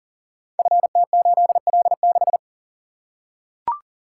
Morse code, heard as a keyed tone, about 42 wpm, 700 Hz. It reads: FT8L6
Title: Code Groups - 5 Letter:Numbers - 40wpm